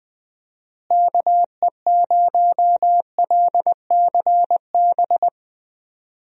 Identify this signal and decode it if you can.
Morse code, heard as a keyed tone, about 20 wpm, 700 Hz. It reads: KE0LCB
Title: USA Call Signs - 6 Characters - Repeated 3x - 20wpm